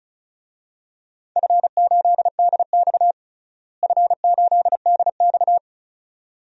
Morse code, heard as a keyed tone, about 36 wpm, 700 Hz. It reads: F8DX F8DX
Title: DXCC-100 Call Signs - 4 Characters - Repeated 2x - 35wpm